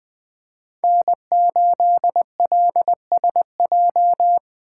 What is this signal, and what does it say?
Morse code, heard as a keyed tone, about 24 wpm, 700 Hz. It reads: N8LSJ
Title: Speed Racing - USA Call Signs 20wpm